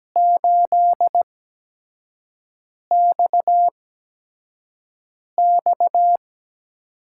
Morse code, heard as a keyed tone, about 17 wpm, 700 Hz. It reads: 8 X X
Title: USA Call Signs - 8x Character Spacing - 17wpm